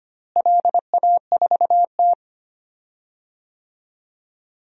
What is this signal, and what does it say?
Morse code, heard as a keyed tone, about 25 wpm, 700 Hz. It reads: LA4T
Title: DXCC-100 Call Signs - 4 Characters - Repeated 3x - 25wpm